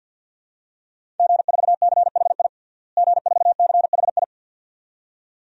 Morse code, heard as a keyed tone, about 52 wpm, 700 Hz. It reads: G4CHI G4CHI
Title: DXCC-100 Call Signs - 5 Characters - Repeated 2x - 50wpm